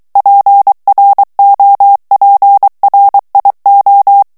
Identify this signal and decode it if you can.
Morse code, heard as a keyed tone, about 23 wpm, 800 Hz. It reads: PROPRIO
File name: ITALIEN texte clair 2 vitesse 22 mots minute